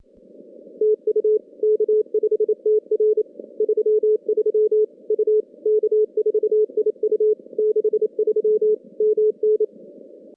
Morse code, heard as a keyed tone, about 29 wpm, 410 Hz. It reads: TU K5TR 33 U K4IU 63 MN
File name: contests / CW-SS-k5tr- -2100